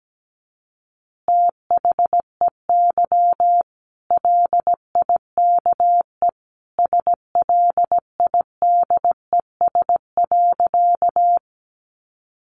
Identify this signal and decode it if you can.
Morse code, heard as a keyed tone, about 17 wpm, 700 Hz. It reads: THEY LIKE SLIDES.
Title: Easy Phrases 17wpm